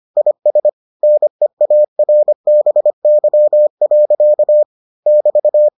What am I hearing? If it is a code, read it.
Morse code, heard as a keyed tone, about 25 wpm, 600 Hz. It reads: IS NEARBY. <BT>